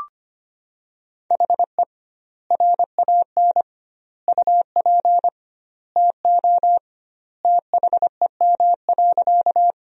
Morse code, heard as a keyed tone, about 25 wpm, 700 Hz. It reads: HE RAN UP TO THEM.
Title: 2x Word-Spacing - Sentences from Top 500 Words - 25wpm